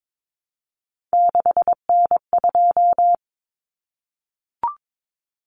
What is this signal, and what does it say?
Morse code, heard as a keyed tone, about 22 wpm, 700 Hz. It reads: 6N2
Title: Common Cut Numbers - 3-Digit 22wpm